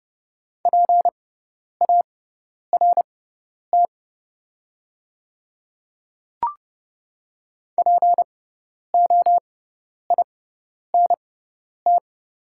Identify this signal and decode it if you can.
Morse code, heard as a keyed tone, about 31 wpm, spaced as for 10 wpm, 690 Hz. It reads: PART POINT
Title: 6x Character Spacing - Top 200 Words - 30wpm